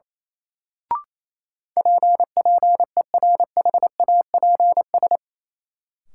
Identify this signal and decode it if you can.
Morse code, heard as a keyed tone, about 28 wpm, 700 Hz. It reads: P PERHAPS
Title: Sentences from Top 1100 Words - New Word Drill Alt - 28wpm